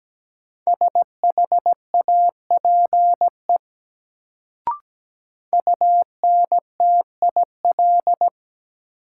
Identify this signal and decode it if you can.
Morse code, heard as a keyed tone, about 17 wpm, 700 Hz. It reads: SHAPE UNTIL
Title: Sentences from Top 500 Words - New Word Drill - 17wpm